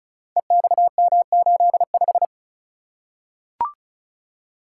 Morse code, heard as a keyed tone, about 36 wpm, 700 Hz. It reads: EXM85
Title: Code Groups - 5 Letter:Numbers - 35wpm